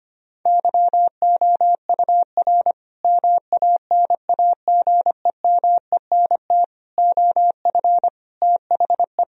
Morse code, heard as a keyed tone, about 25 wpm, 700 Hz. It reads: YOUR MANAGEMENT OF THE
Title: Sentences from Top 1000 Words - Review Part 4 - 25wpm